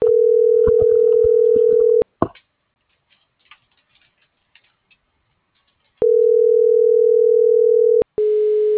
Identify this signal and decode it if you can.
Ringback tone (North America), 440 + 480 Hz.